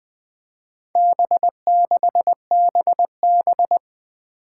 Morse code, heard as a keyed tone, about 20 wpm, 700 Hz. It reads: B6BB
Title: Four Challenging Letter-Number Combinations 20wpm